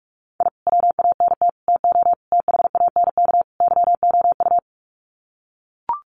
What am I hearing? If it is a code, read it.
Morse code, heard as a keyed tone, about 44 wpm, 700 Hz. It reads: I WANT TO THANK YOU
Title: Sets of 5 Words Encore 45wpm